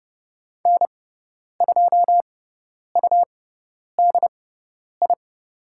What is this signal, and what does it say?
Morse code, heard as a keyed tone, about 30 wpm, spaced as for 10 wpm, 700 Hz. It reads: N2UDI
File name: course / 30:10wpm / 102 Lesson 17g - Letter u - Call Signs 30wpm